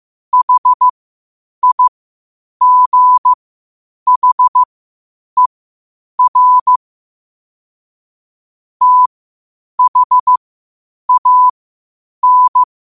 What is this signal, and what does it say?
Morse code, heard as a keyed tone, about 15 wpm, spaced as for 8 wpm, 1000 Hz. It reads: HIGHER THAN
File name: qotdfiles / 8wpm